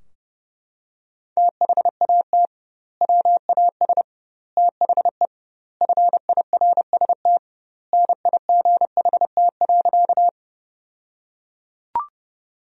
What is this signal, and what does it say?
Morse code, heard as a keyed tone, about 30 wpm, 700 Hz. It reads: THAT WAS THE FIRST NIGHT.
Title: 2x Word-Spacing - Sentences from Top 300 Words - 30wpm